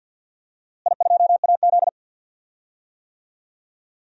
Morse code, heard as a keyed tone, about 47 wpm, 700 Hz. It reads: I2AZ
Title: DXCC-100 Call Signs - 4 Characters - Repeated 2x - 50wpm